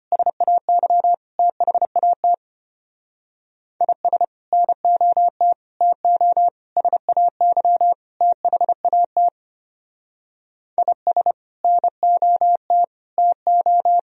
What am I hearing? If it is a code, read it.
Morse code, heard as a keyed tone, about 32 wpm, 700 Hz. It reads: SAY THAT IS NOT TO SAY THAT IS NOT TO